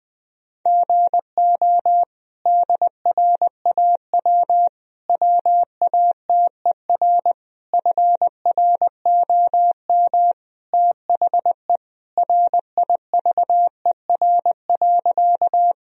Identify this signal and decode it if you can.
Morse code, heard as a keyed tone, about 20 wpm, 700 Hz. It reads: GO DRAW WATER FROM THE RIVER.
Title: Sentences from Top 1000 Words - Review Part 1 - 20wpm